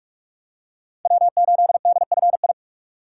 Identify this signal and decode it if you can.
Morse code, heard as a keyed tone, about 44 wpm, 700 Hz. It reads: W8DFI
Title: USA Call Signs - 5 Characters - 6x Character Spacing - 45wpm